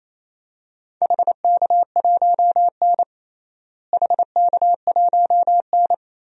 Morse code, heard as a keyed tone, about 28 wpm, 700 Hz. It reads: HK1N HK1N
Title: DXCC-100 Call Signs - 4 Characters - Repeated 2x - 28wpm